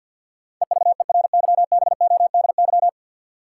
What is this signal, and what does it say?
Morse code, heard as a keyed tone, about 53 wpm, 700 Hz. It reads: EVERYBODY
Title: Sentences from Top 1400 Words - New Word Drill Alt - 50wpm